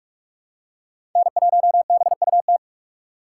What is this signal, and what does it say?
Morse code, heard as a keyed tone, about 46 wpm, 700 Hz. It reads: N1BUT
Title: USA Call Signs - 5 Characters - 3x Character Spacing - 45wpm